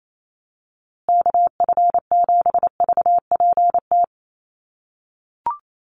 Morse code, heard as a keyed tone, about 28 wpm, 700 Hz. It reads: KF7VPT
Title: USA Call Signs Repeated 3x 28wpm